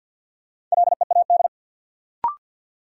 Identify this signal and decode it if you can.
Morse code, heard as a keyed tone, about 50 wpm, 700 Hz. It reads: LEAD
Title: Sentences from Top 500 Words - New Word Drill - 50wpm